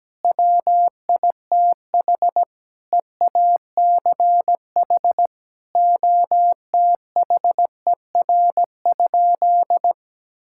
Morse code, heard as a keyed tone, about 17 wpm, 700 Hz. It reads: WITH EACH OTHER?
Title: Sentences from Top 1000 Words - Review Part 6 - 17wpm